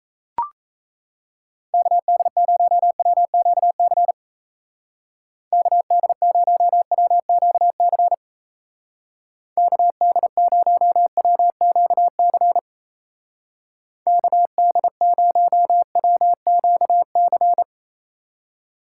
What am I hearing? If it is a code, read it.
Morse code, heard as a keyed tone, about 34 wpm, 700 Hz. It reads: KD0WQC KD0WQC KD0WQC KD0WQC
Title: Speed Racing - USA 2x3 Call Signs 28wpm